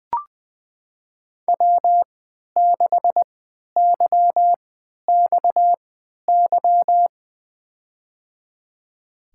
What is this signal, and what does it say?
Morse code, heard as a keyed tone, about 21 wpm, spaced as for 12 wpm, 700 Hz. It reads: W6YXY
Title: USA Call Signs - 5 Characters - 3x Character Spacing - 20wpm